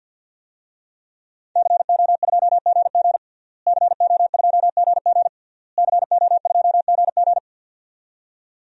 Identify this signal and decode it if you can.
Morse code, heard as a keyed tone, about 53 wpm, 700 Hz. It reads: CO2GG CO2GG CO2GG
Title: DXCC-100 Call Signs - 5 Characters - Repeated 3x - 50wpm